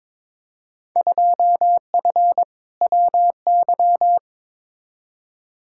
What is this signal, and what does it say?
Morse code, heard as a keyed tone, about 21 wpm, 700 Hz. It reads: 2F WY
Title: Field Day Exchanges Encore - 22wpm